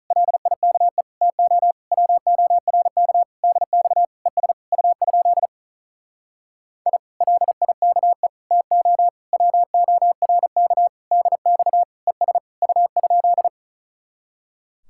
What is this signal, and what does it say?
Morse code, heard as a keyed tone, about 45 wpm, 700 Hz. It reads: LIKE TO WORK DX ES U? I LIKE TO WORK DX ES U?